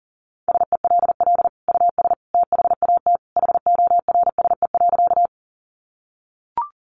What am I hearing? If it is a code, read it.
Morse code, heard as a keyed tone, about 40 wpm, 700 Hz. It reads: SELL US THAT HORSE.
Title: Sentences from Top 1100 Words 40wpm